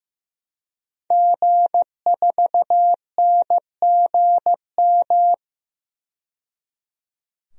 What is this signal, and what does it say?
Morse code, heard as a keyed tone, about 15 wpm, 700 Hz. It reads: G4NGM
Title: DXCC-100 Call Signs - 5 Characters - 15wpm